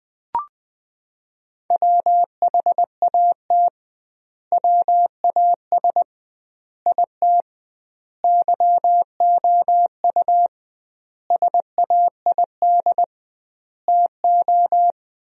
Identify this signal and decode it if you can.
Morse code, heard as a keyed tone, about 20 wpm, 700 Hz. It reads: WHAT WAS IT YOU SAID TO
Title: 2x Word-Spacing - Sentences from Top 200 Words - 20wpm